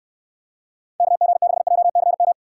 Morse code, heard as a keyed tone, about 64 wpm, 700 Hz. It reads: KG6JCM